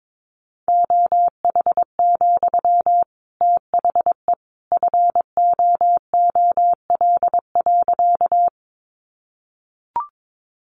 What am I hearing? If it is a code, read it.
Morse code, heard as a keyed tone, about 22 wpm, 700 Hz. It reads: OH, THE FOOL.
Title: A Sound Of Thunder 22wpm